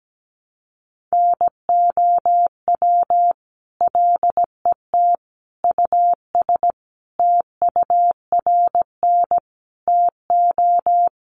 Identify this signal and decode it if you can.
Morse code, heard as a keyed tone, about 17 wpm, 700 Hz. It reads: NOW LET US TURN TO
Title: Sentences from Top 600 Words 17wpm